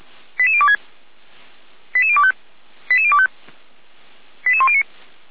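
Five-tone selective call (ZVEI), heard as quick runs of five tones, tones of about 70 ms. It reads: 88026 88025 88025 88109
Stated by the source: ZVEI five-tone call 88192 received by SDR.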